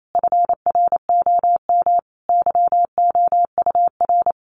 Morse code, heard as a keyed tone, about 28 wpm, 700 Hz. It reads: FROM YOUR
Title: Sentences from Top 1500 Words 28wpm